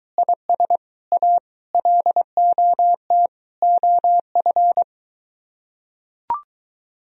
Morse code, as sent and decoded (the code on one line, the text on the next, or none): .. ... / .- / .-.. --- - / --- ..-.
IS A LOT OF